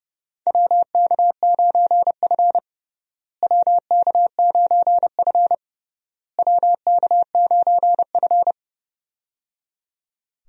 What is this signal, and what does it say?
Morse code, heard as a keyed tone, about 30 wpm, 700 Hz. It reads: WK9F WK9F WK9F
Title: USA Call Signs - 4 Characters - Repeated 3x - 30wpm